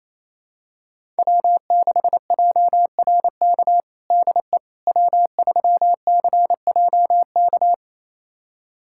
Morse code, heard as a keyed tone, about 28 wpm, 700 Hz. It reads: W6JRK DE W3CJK
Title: USA Callsign DE Callsign - 5 Characters - 28wpm